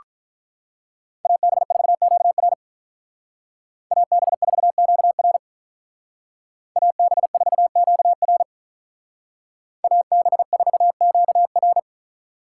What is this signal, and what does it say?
Morse code, heard as a keyed tone, about 44 wpm, 700 Hz. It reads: AB4QR AB4QR AB4QR AB4QR